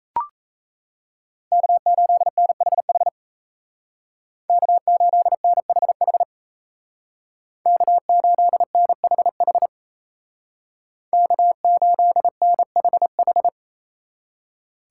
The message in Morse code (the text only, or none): K8NHH K8NHH K8NHH K8NHH